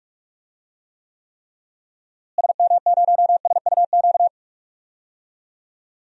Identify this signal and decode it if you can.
Morse code, heard as a keyed tone, about 46 wpm, 700 Hz. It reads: SM0SUQ
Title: DXCC-100 Call Signs - 6 Characters - Repeated 3x - 45wpm